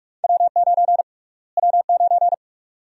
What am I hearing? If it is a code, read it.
Morse code, heard as a keyed tone, about 46 wpm, 700 Hz. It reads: W9 W9
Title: USA Call Sign Prefixes - Repeated 2x - 45wpm